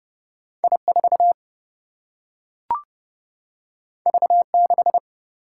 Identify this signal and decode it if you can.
Morse code, heard as a keyed tone, about 30 wpm, 700 Hz. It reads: I 4 V 6
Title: Single-Letter Single-Number 30wpm